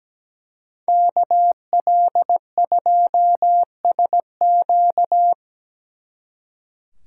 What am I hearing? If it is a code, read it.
Morse code, heard as a keyed tone, about 20 wpm, 700 Hz. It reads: KL2SQ
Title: Speed Racing - USA 2x2 Call Signs 17wpm